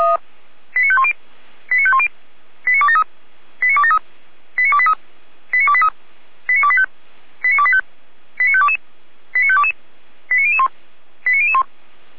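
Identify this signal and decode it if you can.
Five-tone selective call (ZVEI), heard as quick runs of five tones, tones of about 70 ms. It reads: 89510 89510 89273 89273 89283 89283 89285 89285 89522 89522 89001 89001
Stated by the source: ZVEI five-tone selective call received by SDR.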